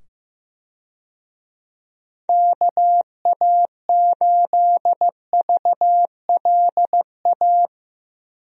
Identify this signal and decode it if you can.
Morse code, heard as a keyed tone, about 15 wpm, 700 Hz. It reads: KA8VLA
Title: USA Call Signs - 6 Characters - 15wpm